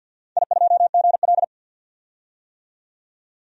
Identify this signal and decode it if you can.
Morse code, heard as a keyed tone, about 52 wpm, 700 Hz. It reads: I2GL
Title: DXCC-100 Call Signs - 4 Characters - 50wpm